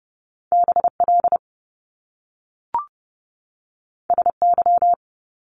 Morse code, heard as a keyed tone, about 30 wpm, 700 Hz. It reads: BL SY
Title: Top Two Letter Combinations in Words 30wpm